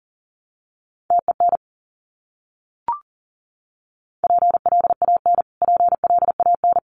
Morse code, heard as a keyed tone, about 40 wpm, 700 Hz. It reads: TEN PLAN PLAN
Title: Top 500 Words Repeated 40wpm